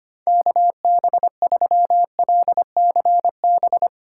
Morse code, heard as a keyed tone, about 25 wpm, 700 Hz. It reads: KB3LCB
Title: USA Call Signs - 6 Characters - 4x Character Spacing - 25wpm